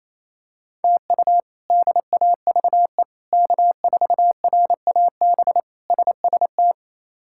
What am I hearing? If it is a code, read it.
Morse code, heard as a keyed tone, about 28 wpm, 700 Hz. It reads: TU DAVE K4RAB SST
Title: K1USN SST QSO Practice - 28wpm